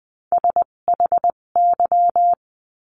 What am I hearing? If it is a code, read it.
Morse code, heard as a keyed tone, about 20 wpm, spaced as for 17 wpm, 700 Hz. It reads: SHY
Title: 257 Lesson 42b - Words 14-20wpm